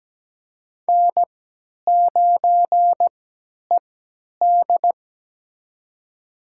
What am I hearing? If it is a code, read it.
Morse code, heard as a keyed tone, about 17 wpm, spaced as for 10 wpm, 700 Hz. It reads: N9ED
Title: USA Call Signs - 4 Characters - 3x Character Spacing - 17wpm